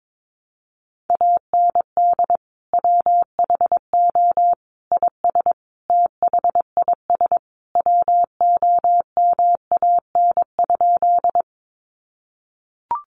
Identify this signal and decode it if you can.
Morse code, heard as a keyed tone, about 22 wpm, 700 Hz. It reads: AND WHO IS THIS WOMAN?
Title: Sentences from Top 1100 Words 22wpm